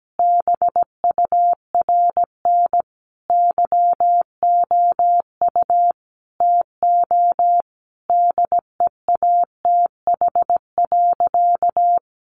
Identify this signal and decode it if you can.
Morse code, heard as a keyed tone, about 17 wpm, 700 Hz. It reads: BURN YOU TO DEATH.
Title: Sentences from Top 1300 Words 17wpm